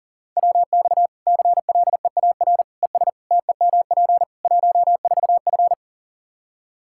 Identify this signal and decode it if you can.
Morse code, heard as a keyed tone, about 37 wpm, 700 Hz. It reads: WX CLEAR ES TEMP 14F
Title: K1USN SST QSO Practice 40wpm